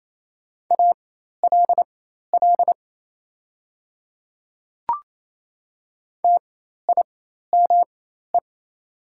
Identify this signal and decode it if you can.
Morse code, heard as a keyed tone, about 28 wpm, spaced as for 13 wpm, 690 Hz. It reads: ALL TIME
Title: 4x Character Spacing - Top 100 Words - 28wpm